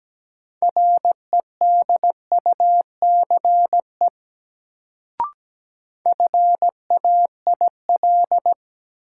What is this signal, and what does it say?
Morse code, heard as a keyed tone, about 17 wpm, 700 Hz. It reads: REDUCE FAIL
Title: Sentences from Top 1100 Words - New Word Drill - 17wpm